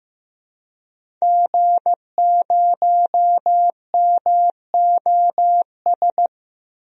Morse code, heard as a keyed tone, about 15 wpm, 700 Hz. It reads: G0MOS
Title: DXCC-100 Call Signs - 5 Characters - 15wpm